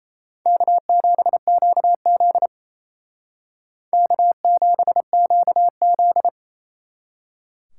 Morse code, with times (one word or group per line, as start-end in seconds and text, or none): 0.5-2.5 K7QZ
3.9-6.3 K7QZ